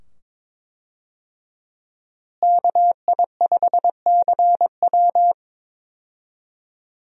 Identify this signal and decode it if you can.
Morse code, heard as a keyed tone, about 21 wpm, 700 Hz. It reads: KI5CW
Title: USA Call Signs - 5 Characters - Repeated 2x - 22wpm